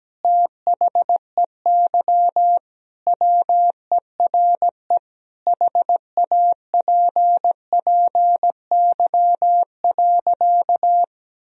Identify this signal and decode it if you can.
Morse code, heard as a keyed tone, about 17 wpm, 700 Hz. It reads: THEY WERE HAPPY.
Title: Sentences from Top 1000 Words - Review Part 1 - 17wpm